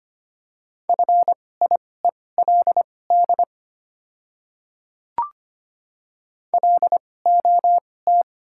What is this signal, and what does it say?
Morse code, heard as a keyed tone, about 25 wpm, spaced as for 18 wpm, 700 Hz. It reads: FIELD LOT
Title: 2x Character Spacing - Top 500 Words - 25wpm